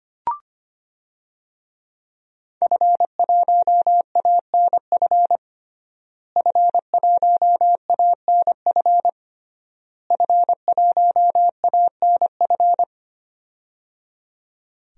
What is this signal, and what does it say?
Morse code, heard as a keyed tone, about 25 wpm, 700 Hz. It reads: F1ANF F1ANF F1ANF
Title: DXCC-100 Call Signs - 5 Characters - Repeated 3x - 25wpm